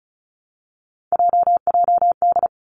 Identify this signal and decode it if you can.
Morse code, heard as a keyed tone, about 36 wpm, 700 Hz. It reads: JJD
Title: Three Challenging Letter-Number Combinations 35wpm